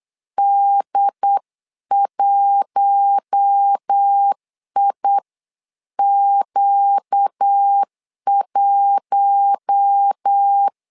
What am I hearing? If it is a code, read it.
Morse code, heard as a keyed tone, about 8 wpm, 800 Hz. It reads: D1IQ1
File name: LEZIONI / 5 LBVC / 5 LBV 8wpm 2837